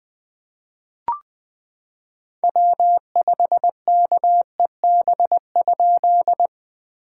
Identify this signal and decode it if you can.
Morse code, heard as a keyed tone, about 20 wpm, 690 Hz. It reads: W5KEB?
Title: POTA QSO Practice - 20wpm